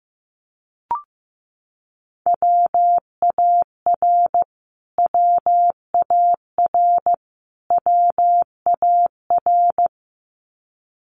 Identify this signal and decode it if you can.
Morse code, heard as a keyed tone, about 15 wpm, 700 Hz. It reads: WAR WAR WAR
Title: Top 500 Words Repeated 15wpm